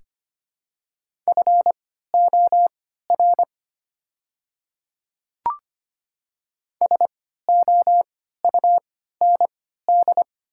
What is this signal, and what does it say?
Morse code, heard as a keyed tone, about 25 wpm, spaced as for 14 wpm, 700 Hz. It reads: FOR SOUND
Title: 3x Character Spacing - Top 300 Words - 25wpm